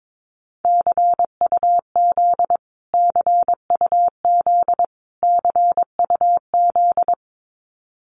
Morse code, heard as a keyed tone, about 22 wpm, 700 Hz. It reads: CUZ CUZ CUZ
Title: Abbreviations Repeated 3x - 22wpm